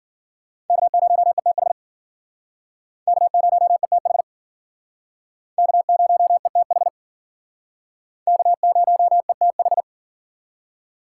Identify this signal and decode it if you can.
Morse code, heard as a keyed tone, about 50 wpm, 700 Hz. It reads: K0ETH K0ETH K0ETH K0ETH
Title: Speed Racing - USA 1x3 Call Signs 40wpm